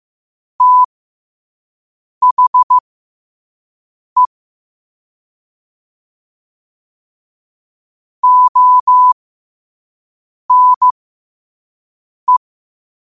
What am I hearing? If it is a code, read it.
Morse code, heard as a keyed tone, about 15 wpm, spaced as for 5 wpm, 1000 Hz. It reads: THE ONE
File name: qotdfiles / 5wpm